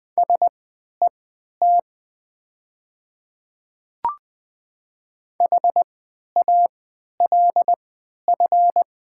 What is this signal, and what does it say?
Morse code, heard as a keyed tone, about 20 wpm, spaced as for 11 wpm, 700 Hz. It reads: SET HALF
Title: 3x Character Spacing - Top 500 Words - 20wpm